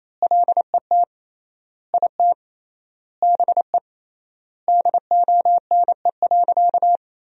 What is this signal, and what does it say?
Morse code, heard as a keyed tone, about 28 wpm, 700 Hz. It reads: LET IT BE DONE.
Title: 3x Word-Spacing - Sentences from Top 500 Words - 28wpm